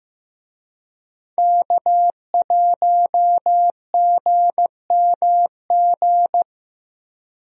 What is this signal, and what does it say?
Morse code, heard as a keyed tone, about 15 wpm, 700 Hz. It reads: K1GMG
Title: USA Call Signs - 5 Characters - Repeated 2x - 15wpm